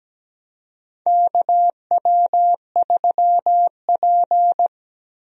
Morse code, heard as a keyed tone, about 17 wpm, 700 Hz. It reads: KW3P
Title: USA Call Signs - 4 Characters - Repeated 3x - 17wpm